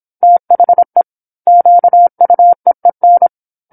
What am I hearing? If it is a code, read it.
Morse code, heard as a keyed tone, about 27 wpm, 700 Hz. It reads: THE QUEEN
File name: 08-Chapter-08 26 WPM